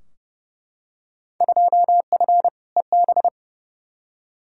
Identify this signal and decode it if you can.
Morse code, heard as a keyed tone, about 30 wpm, 700 Hz. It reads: 2F EB
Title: Field Day Exchanges Encore - 30wpm